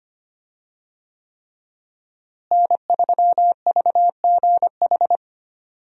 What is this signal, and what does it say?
Morse code, heard as a keyed tone, about 25 wpm, 700 Hz. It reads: N3VGH
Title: USA Call Signs - 5 Characters - Repeated 3x - 25wpm